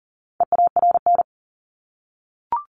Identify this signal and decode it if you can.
Morse code, heard as a keyed tone, about 40 wpm, 700 Hz. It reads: EARN
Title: Top 1251-1500 Words 40wpm